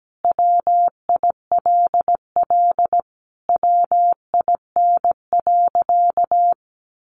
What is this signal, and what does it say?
Morse code, heard as a keyed tone, about 17 wpm, 700 Hz. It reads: WILL WIN.
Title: Sentences from Top 1500 Words 17wpm